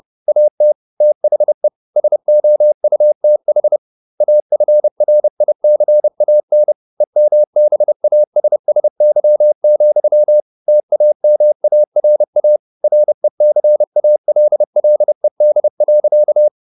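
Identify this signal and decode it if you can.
Morse code, heard as a keyed tone, about 30 wpm, 600 Hz. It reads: AT THE SOUTH AFRICAN EMBASSY, TAMARA RECALLED.
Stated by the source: Morse code at 30 wpm, the novel Never by Ken Follett in English, from CW med Gnister.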